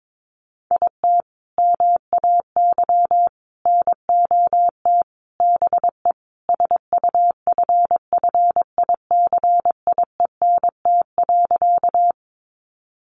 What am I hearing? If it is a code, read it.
Morse code, heard as a keyed tone, about 22 wpm, 700 Hz. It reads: IT MAY NOT BE SUFFICIENT.
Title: Sentences from Top 1400 Words 22wpm